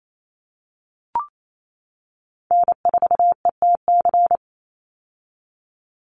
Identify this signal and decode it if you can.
Morse code, heard as a keyed tone, about 28 wpm, 700 Hz. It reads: N4ETC
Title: Call Signs - 28wpm